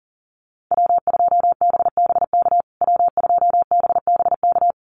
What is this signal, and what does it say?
Morse code, heard as a keyed tone, about 40 wpm, 700 Hz. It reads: W2BBK W2BBK
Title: USA Call Signs Repeated 2x 40wpm